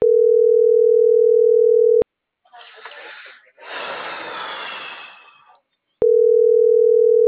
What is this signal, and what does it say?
Ringback tone (North America), 440 + 480 Hz.
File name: CENTRAL MABE / out edwinomarojeda